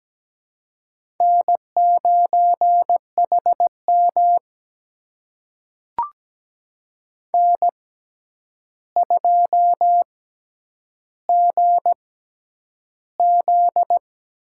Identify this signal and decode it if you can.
Morse code, heard as a keyed tone, about 17 wpm, spaced as for 9 wpm, 700 Hz. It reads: N9HM N2GZ
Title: USA Call Signs - 4 Characters - 6x Character Spacing - 17wpm